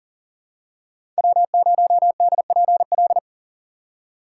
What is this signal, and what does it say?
Morse code, heard as a keyed tone, about 40 wpm, 700 Hz. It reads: W0DPL
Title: USA Call Signs - 5 Characters - 4x Character Spacing - 40wpm